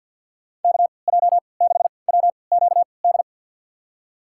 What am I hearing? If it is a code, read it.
Morse code, heard as a keyed tone, about 52 wpm, spaced as for 29 wpm, 700 Hz. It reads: KJ6WQD
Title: USA Call Signs - 3x Character Spacing - 50wpm